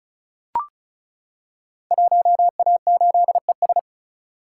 Morse code, heard as a keyed tone, about 36 wpm, 700 Hz. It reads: 1A8ES
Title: Code Groups - 5 Letter:Numbers - 35wpm